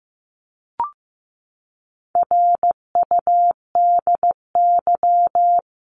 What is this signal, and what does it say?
Morse code, heard as a keyed tone, about 15 wpm, 700 Hz. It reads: RUDY
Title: CWT Exchange 15wpm